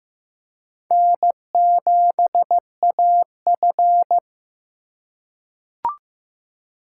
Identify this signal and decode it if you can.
Morse code, heard as a keyed tone, about 15 wpm, 690 Hz. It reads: N7AF
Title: USA Call Signs - 4 Characters - 2x Character Spacing - 15wpm